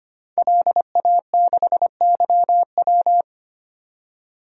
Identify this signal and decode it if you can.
Morse code, heard as a keyed tone, about 25 wpm, 700 Hz. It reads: LA6YW
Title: DXCC-100 Call Signs - 5 Characters - 25wpm